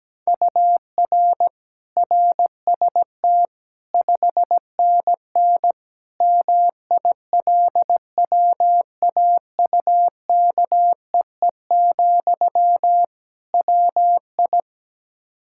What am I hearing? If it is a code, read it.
Morse code, heard as a keyed tone, about 17 wpm, 700 Hz. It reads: UR RST 5NN MILWAUKEE, WI
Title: K1USN SST QSO Practice 17wpm